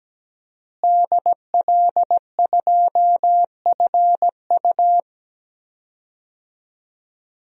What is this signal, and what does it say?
Morse code, heard as a keyed tone, about 17 wpm, 700 Hz. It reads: DL2FU
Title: DXCC-100 Call Signs - 5 Characters - Repeated 2x - 17wpm